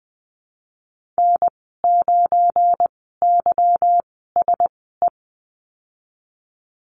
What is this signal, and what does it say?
Morse code, heard as a keyed tone, about 20 wpm, spaced as for 14 wpm, 700 Hz. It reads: N9YSE
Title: 173 Lesson 28g - Number 9 - Call Signs 20wpm